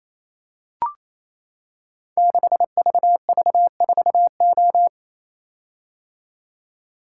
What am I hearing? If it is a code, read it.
Morse code, heard as a keyed tone, about 27 wpm, 700 Hz. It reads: 6VV4O
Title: Code Groups - 5 Letter:Numbers - 28wpm